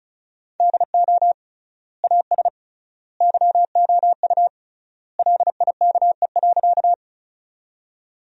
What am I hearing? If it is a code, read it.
Morse code, heard as a keyed tone, about 36 wpm, 700 Hz. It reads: DO AS YOU LIKE.
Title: 3x Word-Spacing - Sentences from Top 100 Words - 35wpm